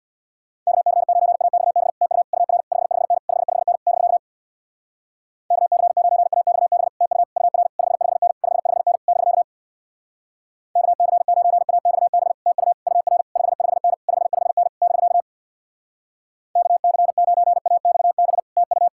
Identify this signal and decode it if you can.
Morse code, heard as a keyed tone, about 66 wpm, 700 Hz. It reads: KC9AXB TU UR 55N 55N <BK> KC9AXB TU UR 55N 55N <BK> KC9AXB TU UR 55N 55N <BK> KC9AXB TU